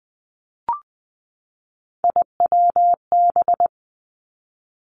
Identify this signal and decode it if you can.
Morse code, heard as a keyed tone, about 19 wpm, 700 Hz. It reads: IWB
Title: Three Challenging Letter-Number Combinations 20wpm